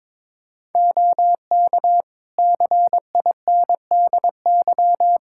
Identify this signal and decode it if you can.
Morse code, heard as a keyed tone, about 22 wpm, 700 Hz. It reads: OK CINDY
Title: K1USN SST QSO Practice 22wpm